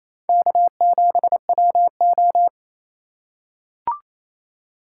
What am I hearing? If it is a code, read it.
Morse code, heard as a keyed tone, about 28 wpm, 700 Hz. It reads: K7WO
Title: USA Call Signs - 4 Characters - 6x Character Spacing - 28wpm